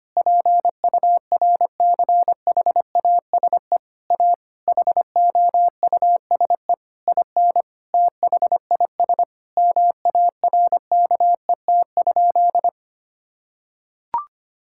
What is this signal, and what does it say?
Morse code, heard as a keyed tone, about 25 wpm, 700 Hz. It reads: PURCHASE A HOUSE IN THIS MARKET?
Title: Sentences from Top 1000 Words - Review Part 5 - 25wpm